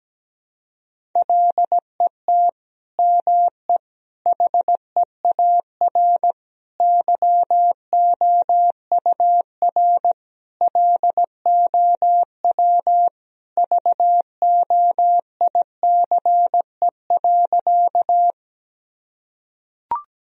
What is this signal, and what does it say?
Morse code, heard as a keyed tone, about 17 wpm, 700 Hz. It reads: LET ME HEAR YOUR LOW VOICE.
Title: Sentences from Top 1000 Words - Review Part 1 - 17wpm